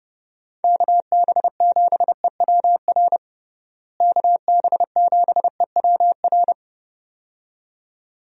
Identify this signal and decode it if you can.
Morse code, heard as a keyed tone, about 30 wpm, 700 Hz. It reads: KB7EWR KB7EWR
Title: USA Call Signs - 6 Characters - Repeated 2x - 30wpm